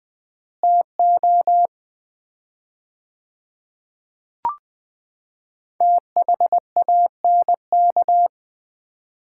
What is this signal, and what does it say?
Morse code, heard as a keyed tone, about 20 wpm, 700 Hz. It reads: TO THANK
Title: 4x Word-Spacing - Sets of 5 Words - 20wpm